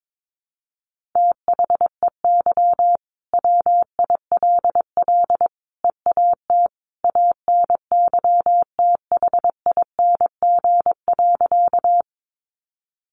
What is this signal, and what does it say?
Morse code, heard as a keyed tone, about 22 wpm, 700 Hz. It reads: THEY WILL EAT ANYTHING.
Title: Sentences from Top 1000 Words 22wpm